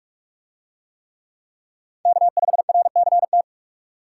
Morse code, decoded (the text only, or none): K5RCT